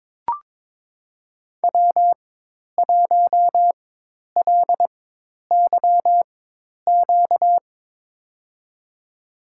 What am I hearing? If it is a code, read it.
Morse code, heard as a keyed tone, about 22 wpm, spaced as for 10 wpm, 700 Hz. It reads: W1LYQ
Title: USA Call Signs - 4x Character Spacing - 22wpm